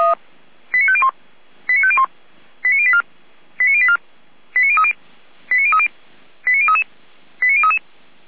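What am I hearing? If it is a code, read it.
Five-tone selective call (ZVEI), heard as quick runs of five tones, tones of about 70 ms. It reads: 89501 89501 89084 89084 89030 89030 89033 89033
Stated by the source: ZVEI five-tone selective call received on SDR.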